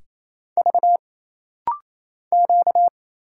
Morse code, heard as a keyed tone, about 27 wpm, 700 Hz. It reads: V Q
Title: Single Letters - Rapid-Fire 28wpm